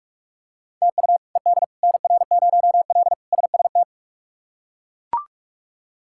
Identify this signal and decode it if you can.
Morse code, heard as a keyed tone, about 42 wpm, 700 Hz. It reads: TU ED NR0L SST
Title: K1USN SST QSO Practice - 45wpm